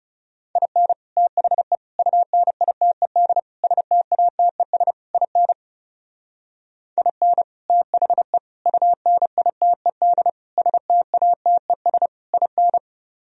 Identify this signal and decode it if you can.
Morse code, heard as a keyed tone, about 40 wpm, 700 Hz. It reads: IN THE UNITED STATES IN IN THE UNITED STATES IN